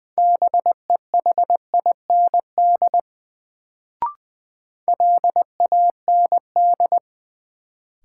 Morse code, heard as a keyed tone, about 21 wpm, 700 Hz. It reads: BEHIND LAND
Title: Sentences from Top 500 Words - New Word Drill - 20wpm